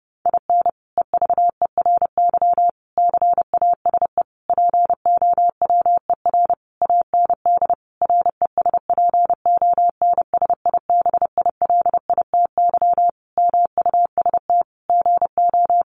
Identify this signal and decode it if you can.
Morse code, heard as a keyed tone, about 30 wpm, 700 Hz. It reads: IN EVERY CASE POWER AND RESPONSIBILITY MUST GO
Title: Sentences from Top 1200 Words 30wpm